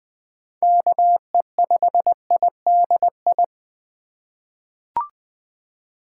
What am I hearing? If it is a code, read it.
Morse code, heard as a keyed tone, about 20 wpm, 700 Hz. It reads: KE5IDI
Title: USA Call Signs - 6 Characters - 4x Character Spacing - 20wpm